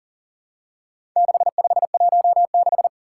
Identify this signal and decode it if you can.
Morse code, heard as a keyed tone, about 40 wpm, 700 Hz. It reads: 6516
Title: Four Challenging Letter-Number Combinations 40wpm